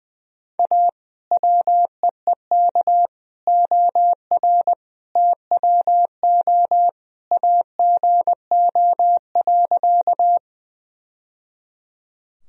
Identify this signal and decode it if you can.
Morse code, heard as a keyed tone, about 21 wpm, 700 Hz. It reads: A WEEK OR TWO AGO.
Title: Sentences from Top 1000 Words - Review Part 3 - 20wpm